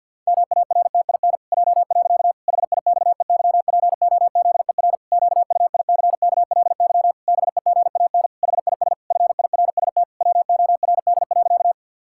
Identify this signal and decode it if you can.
Morse code, heard as a keyed tone, about 51 wpm, 700 Hz. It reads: MARTIN J. HIXEYPOZER QUICKLY BEGAN HIS FIRST WORD.
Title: Pangram Challenge v2 - 50wpm